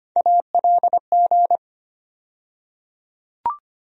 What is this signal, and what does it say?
Morse code, heard as a keyed tone, about 25 wpm, 700 Hz. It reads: ALG
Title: USA Call Sign Postfixes - 3 Letters 25wpm